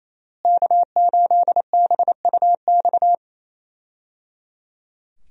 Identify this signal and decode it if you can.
Morse code, heard as a keyed tone, about 28 wpm, 700 Hz. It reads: K8BUX
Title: USA Call Signs - 5 Characters - Repeated 3x - 28wpm